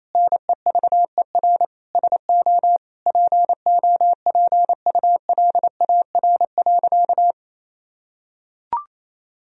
Morse code, heard as a keyed tone, about 28 wpm, 700 Hz. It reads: NEVER SO POPULAR.
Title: Sentences from Top 1000 Words - Review Part 3 - 28wpm